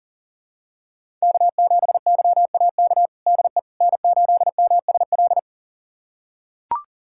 Morse code, heard as a keyed tone, about 40 wpm, 700 Hz. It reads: K7YAK DE N8MSL
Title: USA Callsign DE Callsign - 5 Characters - 40wpm